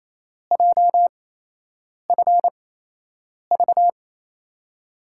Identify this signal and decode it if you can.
Morse code, heard as a keyed tone, about 28 wpm, spaced as for 8 wpm, 700 Hz. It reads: JFV
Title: USA Call Signs - 8x Character Spacing - 28wpm